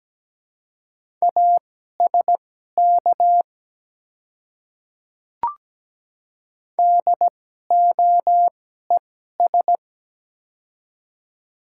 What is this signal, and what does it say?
Morse code, heard as a keyed tone, about 17 wpm, spaced as for 12 wpm, 700 Hz. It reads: ASK DOES
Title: 2x Character Spacing - Top 200 Words - 17wpm